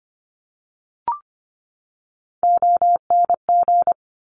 Morse code, heard as a keyed tone, about 25 wpm, 700 Hz. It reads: ONG
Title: Top Three Letter Combinations in Words 25wpm